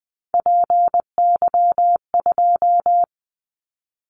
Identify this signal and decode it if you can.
Morse code, heard as a keyed tone, about 19 wpm, 700 Hz. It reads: PY2
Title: Callsign Prefixes for Top 50 DXCC - 20wpm